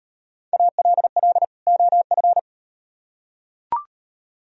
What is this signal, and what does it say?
Morse code, heard as a keyed tone, about 37 wpm, 700 Hz. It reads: ALL OF